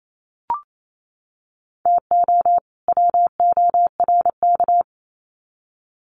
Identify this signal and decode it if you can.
Morse code, heard as a keyed tone, about 28 wpm, 700 Hz. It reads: TO WORK
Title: Sets of 2 Words 28wpm